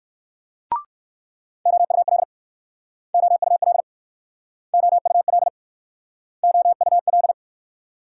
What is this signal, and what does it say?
Morse code, heard as a keyed tone, about 59 wpm, 690 Hz. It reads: OUL OUL OUL OUL